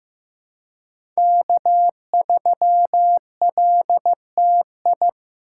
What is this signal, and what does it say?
Morse code, heard as a keyed tone, about 15 wpm, 700 Hz. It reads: K3LTI
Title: USA Call Signs - 5 Characters - 4x Character Spacing - 15wpm